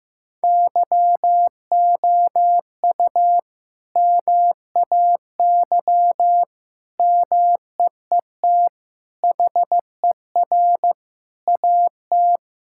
Morse code, heard as a keyed tone, about 16 wpm, 700 Hz. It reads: YOU MAY MEET HER AT
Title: Sentences from Top 1600 Words 15wpm